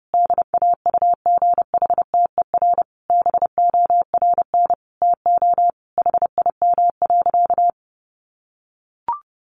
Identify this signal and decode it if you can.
Morse code, heard as a keyed tone, about 30 wpm, 700 Hz. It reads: DAUGHTER BORN TO HIM.
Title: Sentences from Top 1500 Words 30wpm